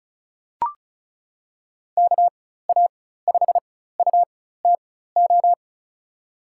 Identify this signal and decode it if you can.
Morse code, heard as a keyed tone, about 36 wpm, spaced as for 16 wpm, 700 Hz. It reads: KA5UTO
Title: USA Call Signs - 6 Characters - 4x Character Spacing - 35wpm